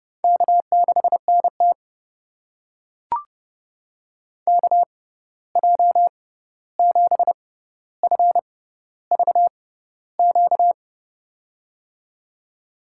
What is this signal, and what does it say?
Morse code, heard as a keyed tone, about 32 wpm, spaced as for 14 wpm, 700 Hz. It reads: K6NT KJ7FVQ
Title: USA Call Signs - 6x Character Spacing - 30wpm